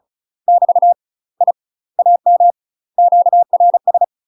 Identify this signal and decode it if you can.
Morse code, heard as a keyed tone, about 36 wpm, 700 Hz. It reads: <BT> I AM QRS
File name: UK MASTER QSOs TRAINING / 35WPM / UK MASTER QSOs 0007